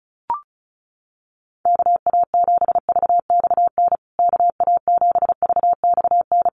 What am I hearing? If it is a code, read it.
Morse code, heard as a keyed tone, about 34 wpm, 700 Hz. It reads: KA7VXN KA7VXN
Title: USA Call Signs Repeated 3x 35wpm